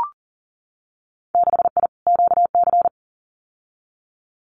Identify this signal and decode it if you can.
Morse code, heard as a keyed tone, about 40 wpm, 700 Hz. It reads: 6I QC
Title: Winter Field Day Exchanges Encore 40wpm